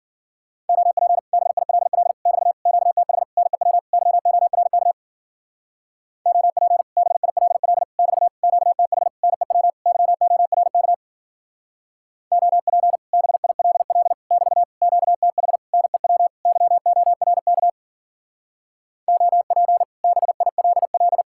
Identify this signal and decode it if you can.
Morse code, heard as a keyed tone, about 50 wpm, 700 Hz. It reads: OP BILL <BT> QTH NEW YORK OP BILL <BT> QTH NEW YORK OP BILL <BT> QTH NEW YORK OP BILL